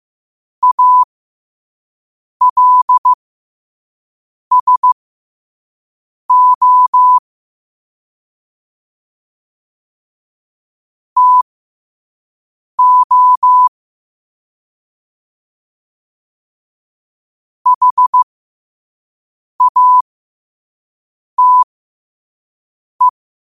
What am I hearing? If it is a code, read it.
Morse code, heard as a keyed tone, about 15 wpm, spaced as for 5 wpm, 1000 Hz. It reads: ALSO TO HATE
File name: qotdfiles / 5wpm